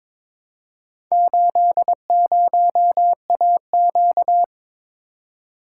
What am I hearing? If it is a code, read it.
Morse code, heard as a keyed tone, about 22 wpm, 700 Hz. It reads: 80AQ
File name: practice / 22wpm / Four Letter-Number Combinations - 22wpm